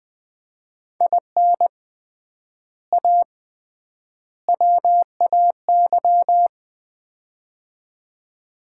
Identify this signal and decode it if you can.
Morse code, heard as a keyed tone, about 20 wpm, 700 Hz. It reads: IN A WAY
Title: 3x Word-Spacing - Sets of 3 Words - 20wpm